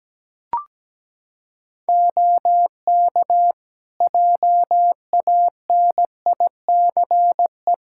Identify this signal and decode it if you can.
Morse code, heard as a keyed tone, about 17 wpm, 700 Hz. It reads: OK JANICE
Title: K1USN SST QSO Practice 17wpm